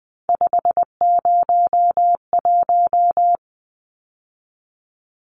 Morse code, heard as a keyed tone, about 20 wpm, 700 Hz. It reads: 501
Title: Numbers - 3-Digit 20wpm